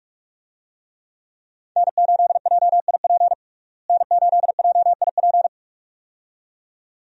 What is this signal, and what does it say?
Morse code, heard as a keyed tone, about 46 wpm, 700 Hz. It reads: N8JIP N8JIP
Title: USA Call Signs - 5 Characters - Repeated 2x - 45wpm